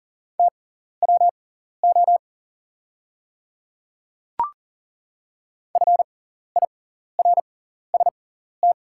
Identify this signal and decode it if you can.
Morse code, heard as a keyed tone, about 40 wpm, spaced as for 14 wpm, 690 Hz. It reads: TWO FIRST
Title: 6x Character Spacing - Top 100 Words - 40wpm